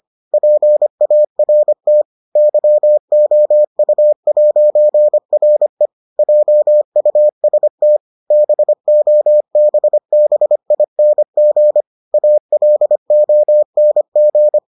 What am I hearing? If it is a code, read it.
Morse code, heard as a keyed tone, about 25 wpm, 600 Hz. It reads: PART YOU'RE JUST BOBBING ALONG